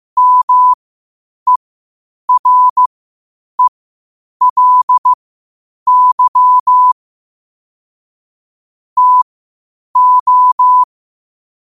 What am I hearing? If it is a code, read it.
Morse code, heard as a keyed tone, about 15 wpm, spaced as for 8 wpm, 1000 Hz. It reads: MERELY TO
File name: qotdfiles / 8wpm